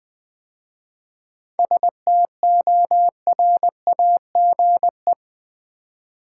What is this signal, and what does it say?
Morse code, heard as a keyed tone, about 20 wpm, 700 Hz. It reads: STORAGE
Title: 7 Letter Words - First 3 Letters Given - 20wpm